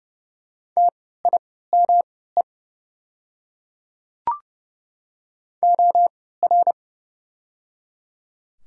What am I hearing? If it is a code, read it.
Morse code, heard as a keyed tone, about 30 wpm, spaced as for 17 wpm, 700 Hz. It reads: TIME OR
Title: 3x Character Spacing - Top 100 Words - 30wpm